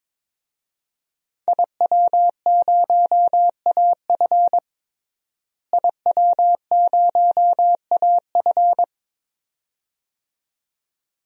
Morse code, heard as a keyed tone, about 22 wpm, 700 Hz. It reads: IW0AF IW0AF
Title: DXCC-100 Call Signs - 5 Characters - Repeated 2x - 22wpm